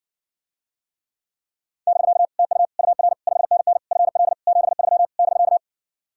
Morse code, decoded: <BK> TU UR 5NN FL 73 <BK>

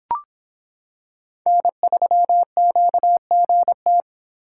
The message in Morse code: N3QGT

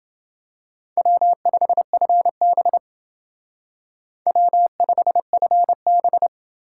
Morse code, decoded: W5FB W5FB

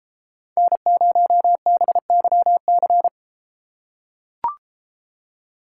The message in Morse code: N0BYC

